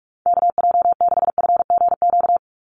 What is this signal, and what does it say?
Morse code, heard as a keyed tone, about 46 wpm, 700 Hz. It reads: KJ6FGQ
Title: USA Call Signs 45wpm